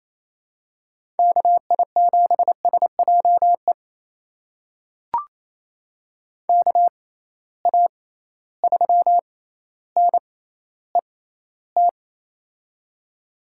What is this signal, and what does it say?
Morse code, heard as a keyed tone, about 28 wpm, spaced as for 14 wpm, 700 Hz. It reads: KI7SJE KA3NET
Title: USA Call Signs - 6 Characters - 6x Character Spacing - 28wpm